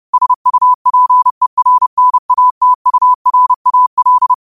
Morse code, heard as a keyed tone, about 30 wpm, 1000 Hz. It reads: SUPERNATURAL